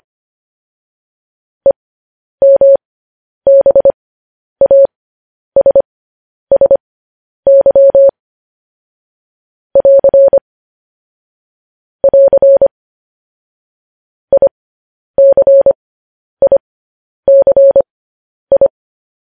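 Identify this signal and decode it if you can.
Morse code, heard as a keyed tone, about 25 wpm, spaced as for 10 wpm, 550 Hz. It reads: EMBASSY <AR> <AR> ICICI
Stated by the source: Morse code news headlines broadcast.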